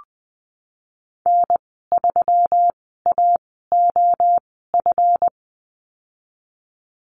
Morse code, heard as a keyed tone, about 20 wpm, spaced as for 15 wpm, 700 Hz. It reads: N3AOF
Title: 161 Lesson 26g - Letter g - Call Signs 20wpm